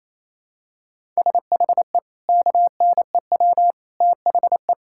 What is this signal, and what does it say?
Morse code, heard as a keyed tone, about 28 wpm, 700 Hz. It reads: SHE KNEW THE
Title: Sentences from Top 1000 Words - Review Part 6 - 28wpm